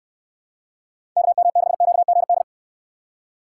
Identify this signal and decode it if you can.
Morse code, heard as a keyed tone, about 64 wpm, 700 Hz. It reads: KM6QGD